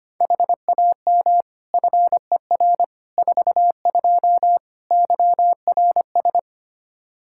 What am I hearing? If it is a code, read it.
Morse code, heard as a keyed tone, about 25 wpm, 700 Hz. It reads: HAM FER 42 YRS
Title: Practice QSO - Exchange 3 - More Info - 25wpm